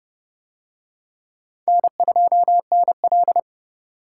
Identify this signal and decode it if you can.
Morse code, heard as a keyed tone, about 28 wpm, 700 Hz. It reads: N2NL